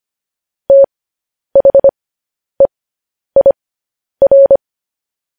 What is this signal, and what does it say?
Morse code, heard as a keyed tone, about 25 wpm, spaced as for 10 wpm, 550 Hz. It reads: THEIR
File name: podcast / News headlines-1x-25x10